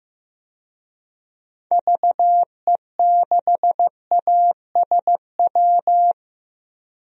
Morse code, heard as a keyed tone, about 15 wpm, 700 Hz. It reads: VE6ASW
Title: DXCC-100 Call Signs - 6 Characters - Repeated 2x - 15wpm